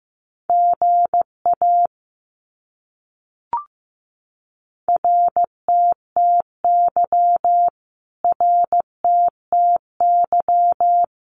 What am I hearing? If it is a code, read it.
Morse code, heard as a keyed tone, about 15 wpm, 700 Hz. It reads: GA RTTY RTTY
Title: Abbreviations Repeated 3x - 15wpm